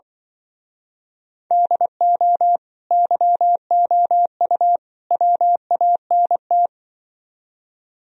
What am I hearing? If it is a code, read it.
Morse code, heard as a keyed tone, about 24 wpm, 700 Hz. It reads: DO YOU WANT
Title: Speed-Racing - Sets of 3 Words 20wpm